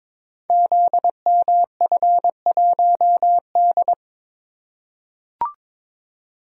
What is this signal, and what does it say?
Morse code, heard as a keyed tone, about 22 wpm, 700 Hz. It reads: ZMF1D
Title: Code Groups - 5 Letter:Numbers - 22wpm